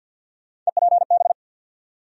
Morse code, heard as a keyed tone, about 52 wpm, 700 Hz. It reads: EPB
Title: USA Call Sign Postfixes - 3 Letters 50wpm